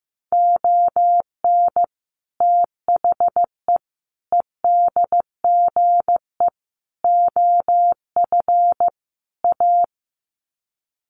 Morse code, heard as a keyed tone, about 15 wpm, 700 Hz. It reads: ON THE EDGE OF A
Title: Sets of 5 Words Encore2 15wpm